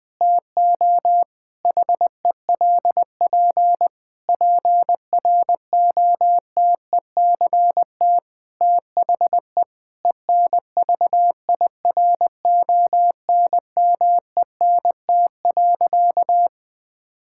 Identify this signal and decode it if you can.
Morse code, heard as a keyed tone, about 20 wpm, 700 Hz. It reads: TO HELP PROTECT THE ENVIRONMENT.
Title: Difficult Sentences with Hints - 20wpm